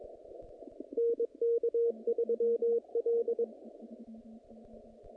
Morse code, heard as a keyed tone, about 22 wpm, 470 Hz. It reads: NK3L
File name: contests / naqp-cw-k5tr-op-k5pi- -2300